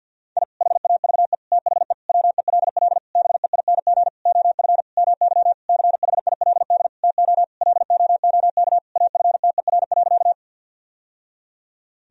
Morse code, heard as a keyed tone, about 50 wpm, 700 Hz. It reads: I HAVE THE WELL BEING OF MY CHILD TO LOOK AFTER.
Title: Sentences from Top 1000 Words - Review Part 3 - 50wpm